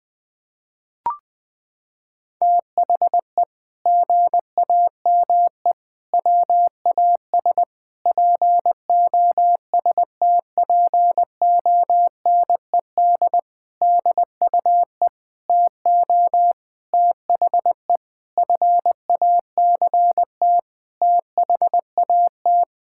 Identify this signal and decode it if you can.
Morse code, heard as a keyed tone, about 20 wpm, 700 Hz. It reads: THE GAME WAS POSTPONED DUE TO THE FACT THAT
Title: Sentences with 6-Word Phrases - 20wpm